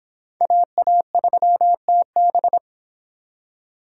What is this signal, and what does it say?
Morse code, heard as a keyed tone, about 28 wpm, 700 Hz. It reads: AA3TB